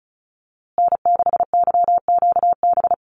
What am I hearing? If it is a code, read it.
Morse code, heard as a keyed tone, about 34 wpm, 700 Hz. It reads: N6YQB
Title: USA Call Signs Repeated 2x 35wpm